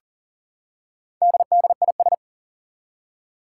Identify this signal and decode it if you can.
Morse code, heard as a keyed tone, about 40 wpm, 700 Hz. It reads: DDIS
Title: Four Challenging Letter-Number Combinations 40wpm